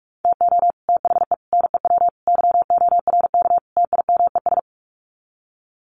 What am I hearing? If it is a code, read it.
Morse code, heard as a keyed tone, about 44 wpm, 700 Hz. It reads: TO THE NEW YORK TIMES
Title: Sets of 5 Words Encore2 45wpm